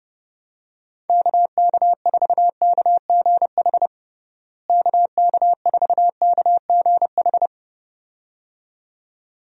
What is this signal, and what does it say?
Morse code, heard as a keyed tone, about 32 wpm, 700 Hz. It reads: KK4KGH KK4KGH
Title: USA Call Signs - 6 Characters - Repeated 2x - 30wpm